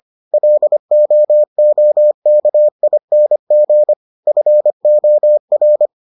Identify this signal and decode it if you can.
Morse code, heard as a keyed tone, about 25 wpm, 600 Hz. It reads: LOOKING FOR